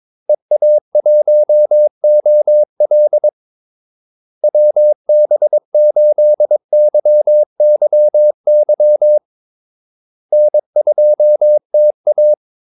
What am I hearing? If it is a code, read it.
Morse code, heard as a keyed tone, about 22 wpm, 600 Hz. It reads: EA1OL WB8YYY N2TA
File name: Callsigns-22wpm